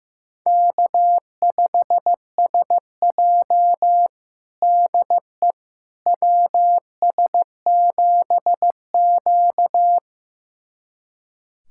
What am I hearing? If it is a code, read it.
Morse code, heard as a keyed tone, about 15 wpm, 700 Hz. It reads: K5SJ DE WS7Q
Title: USA Callsign DE Callsign - 4 Characters - 15wpm